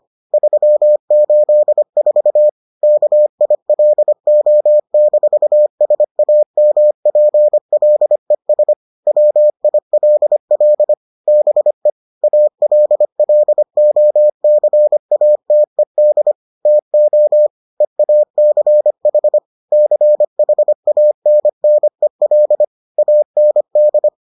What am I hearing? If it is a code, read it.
Morse code, heard as a keyed tone, about 25 wpm, 600 Hz. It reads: 384 KILO-SAMPLES WILL BE ALLOCATED TO EACH CHANNEL AND